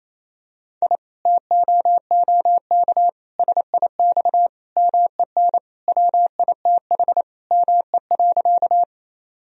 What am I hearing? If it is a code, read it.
Morse code, heard as a keyed tone, about 28 wpm, 700 Hz. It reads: I TOOK SIX MEN WITH ME.
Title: Sentences from Top 1000 Words - Review Part 2 - 28wpm